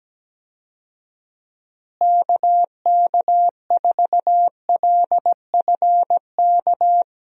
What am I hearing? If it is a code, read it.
Morse code, heard as a keyed tone, about 17 wpm, 700 Hz. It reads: KK4LFK
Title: USA Call Signs - 6 Characters - Repeated 3x - 17wpm